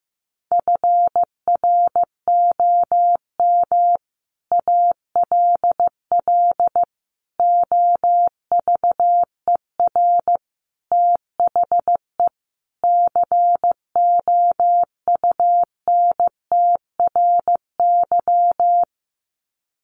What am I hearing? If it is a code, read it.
Morse code, heard as a keyed tone, about 15 wpm, 700 Hz. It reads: FROM ALL OVER THE COUNTRY
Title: Sets of 5 Words Encore 15wpm